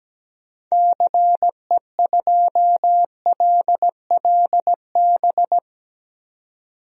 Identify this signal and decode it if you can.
Morse code, heard as a keyed tone, about 17 wpm, 700 Hz. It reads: CE2LLB
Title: DXCC-100 Call Signs - 6 Characters - 17wpm